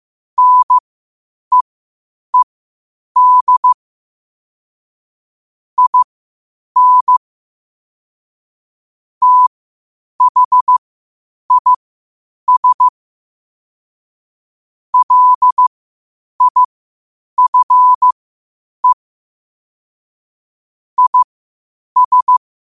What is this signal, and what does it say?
Morse code, heard as a keyed tone, about 15 wpm, spaced as for 8 wpm, 1000 Hz. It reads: NEED IN THIS LIFE IS
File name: qotdfiles / 8wpm